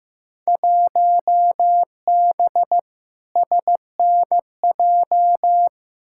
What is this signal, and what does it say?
Morse code, heard as a keyed tone, about 15 wpm, 700 Hz. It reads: 1B SNJ
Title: Field Day Exchanges - 15wpm